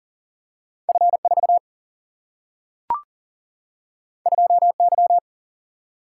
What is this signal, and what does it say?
Morse code, heard as a keyed tone, about 38 wpm, 700 Hz. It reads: F 4 2 Y
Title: Single-Letter Single-Number 40wpm